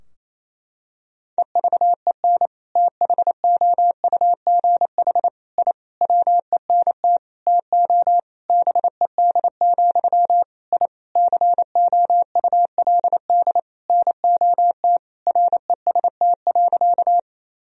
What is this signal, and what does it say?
Morse code, heard as a keyed tone, about 28 wpm, 700 Hz. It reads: EVEN THOUGH I WENT TO BED, I COULD NOT REST.
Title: Sentences from Top 1000 Words - Review Part 2 - 28wpm